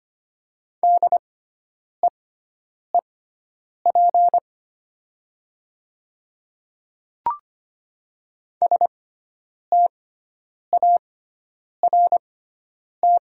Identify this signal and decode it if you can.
Morse code, heard as a keyed tone, about 25 wpm, spaced as for 9 wpm, 700 Hz. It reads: DEEP START
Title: 6x Character Spacing - Top 500 Words - 25wpm